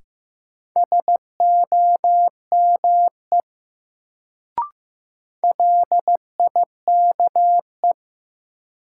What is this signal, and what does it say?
Morse code, heard as a keyed tone, about 15 wpm, 700 Hz. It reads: SOME LIKE
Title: Sentences from Top 100 Words - New Word Drill - 15wpm